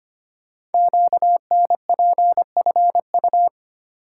Morse code, heard as a keyed tone, about 25 wpm, 700 Hz. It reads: QNPFU
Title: Code Groups - 5 Letters - 25wpm